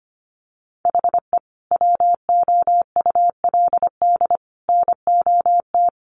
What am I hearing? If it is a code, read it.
Morse code, heard as a keyed tone, about 25 wpm, 700 Hz. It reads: HE WOULD NOT
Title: Sentences from Top 1500 Words 25wpm